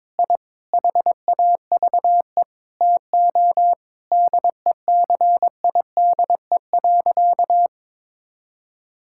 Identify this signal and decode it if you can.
Morse code, heard as a keyed tone, about 22 wpm, 700 Hz. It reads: I HAVE TO DECIDE.
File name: practice / 22wpm / Sentences with 8-Word Phrases - 22wpm